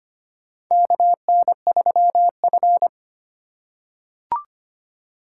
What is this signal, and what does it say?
Morse code, heard as a keyed tone, about 25 wpm, 700 Hz. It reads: KN3F
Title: USA Call Signs - 4 Characters - 8x Character Spacing - 25wpm